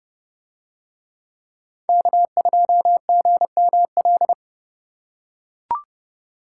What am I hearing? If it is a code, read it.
Morse code, heard as a keyed tone, about 31 wpm, 700 Hz. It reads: K2GML